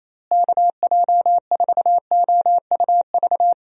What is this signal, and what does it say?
Morse code, heard as a keyed tone, about 28 wpm, 700 Hz. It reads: KJ4OUV
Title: USA Call Signs - 6 Characters - Repeated 3x - 28wpm